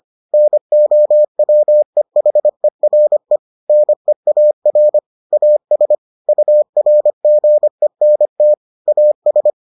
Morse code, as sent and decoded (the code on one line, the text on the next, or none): -. --- .-- . .... .-. / -. .- .-. / .- ... / ..- .-. --. -. - / .- ...
NOWEHERE NEAR AS URGENT AS